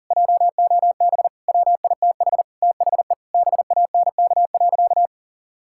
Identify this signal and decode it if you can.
Morse code, heard as a keyed tone, about 40 wpm, 700 Hz. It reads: JOB WITH THE BANK.
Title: Sentences from Top 1000 Words - Review Part 6 - 40wpm